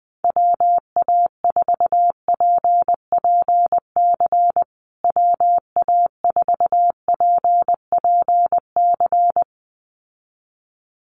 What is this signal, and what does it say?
Morse code, heard as a keyed tone, about 20 wpm, 700 Hz. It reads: WA4PPC WA4PPC
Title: USA Call Signs Repeated 2x 20wpm